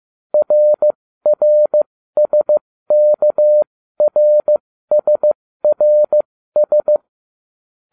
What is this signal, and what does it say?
Morse code, heard as a keyed tone, about 15 wpm, spaced as for 13 wpm, 600 Hz. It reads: RRSKRSRS